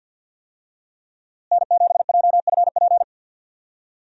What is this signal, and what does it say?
Morse code, heard as a keyed tone, about 50 wpm, 700 Hz. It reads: N7JFP
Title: USA Call Signs - 5 Characters - Repeated 3x - 50wpm